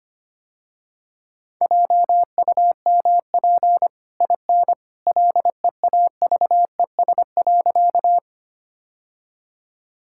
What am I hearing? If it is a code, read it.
Morse code, heard as a keyed tone, about 25 wpm, 700 Hz. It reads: JUMP IN LEAVES.
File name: Easy Sentences with Hints - 25wpm